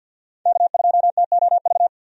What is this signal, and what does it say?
Morse code, heard as a keyed tone, about 52 wpm, 700 Hz. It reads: K2TOV
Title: USA Call Signs - 5 Characters - 50wpm